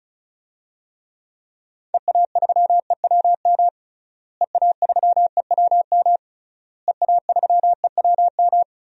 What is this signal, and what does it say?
Morse code, heard as a keyed tone, about 36 wpm, 700 Hz. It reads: EA3EWM EA3EWM EA3EWM
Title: DXCC-100 Call Signs - 6 Characters - Repeated 3x - 35wpm